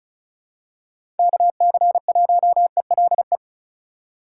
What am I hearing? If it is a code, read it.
Morse code, heard as a keyed tone, about 36 wpm, 700 Hz. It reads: KC1ELE
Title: USA Call Signs - 6 Characters - 4x Character Spacing - 35wpm